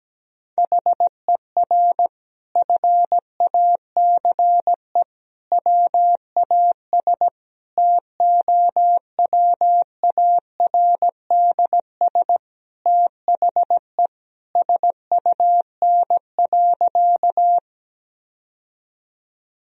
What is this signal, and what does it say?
Morse code, heard as a keyed tone, about 17 wpm, 700 Hz. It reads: HER FACE WAS TOWARDS THE SUN.
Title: Sentences from Top 1000 Words - Review Part 2 - 17wpm